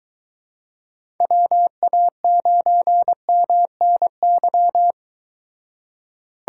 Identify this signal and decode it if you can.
Morse code, heard as a keyed tone, about 23 wpm, 700 Hz. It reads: WA9MNY